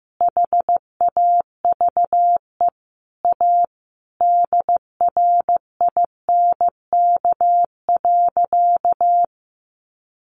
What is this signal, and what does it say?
Morse code, heard as a keyed tone, about 15 wpm, 700 Hz. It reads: HAVE A DRINK.
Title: Sentences from Top 1500 Words 15wpm